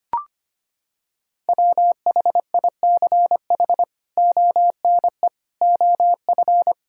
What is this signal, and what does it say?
Morse code, heard as a keyed tone, about 25 wpm, 700 Hz. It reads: WHICH ONE OF
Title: Sentences from Top 1000 Words - Review Part 5 - 25wpm